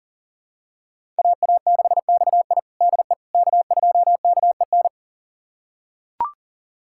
Morse code, heard as a keyed tone, about 40 wpm, 700 Hz. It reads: AA6XI DE K2KEN
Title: USA Callsign DE Callsign - 5 Characters - 40wpm